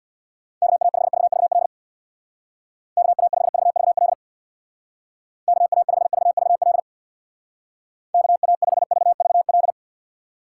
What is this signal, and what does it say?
Morse code, heard as a keyed tone, about 63 wpm, 700 Hz. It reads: KA5VVL KA5VVL KA5VVL KA5VVL